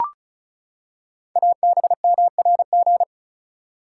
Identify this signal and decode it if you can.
Morse code, heard as a keyed tone, about 34 wpm, 690 Hz. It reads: ABMRG
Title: Code Groups - 5 Letters - 35wpm